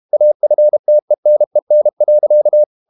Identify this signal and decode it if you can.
Morse code, heard as a keyed tone, about 32 wpm, 600 Hz. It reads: AFTENEN.
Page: Ferie 32 wpm